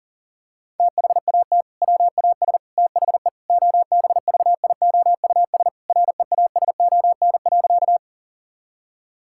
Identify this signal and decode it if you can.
Morse code, heard as a keyed tone, about 40 wpm, 700 Hz. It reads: THAT WAS THE OBVIOUS REASON.
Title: Sentences from Top 1600 Words 40wpm